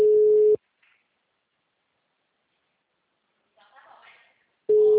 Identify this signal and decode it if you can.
Ringback tone (Europe, 425 Hz), with speech.